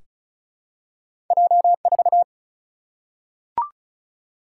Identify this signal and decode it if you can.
Morse code, heard as a keyed tone, about 35 wpm, 700 Hz. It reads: J4
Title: Single-Letter Single-Number 35wpm